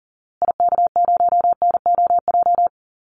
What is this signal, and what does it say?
Morse code, heard as a keyed tone, about 40 wpm, 700 Hz. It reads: IK0NOJ
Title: Call Signs Repeated 2x - 40wpm